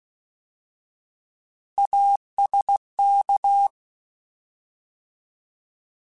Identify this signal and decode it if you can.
Morse code, heard as a keyed tone, about 16 wpm, 800 Hz. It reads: ASK